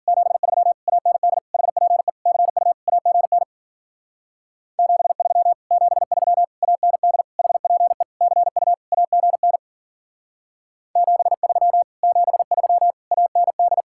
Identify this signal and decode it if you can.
Morse code, heard as a keyed tone, about 48 wpm, 700 Hz. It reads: 73 AND HPE CU AGN 73 73 AND HPE CU AGN 73 73 AND
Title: Speed-Racing Common QSO Phrases - 40wpm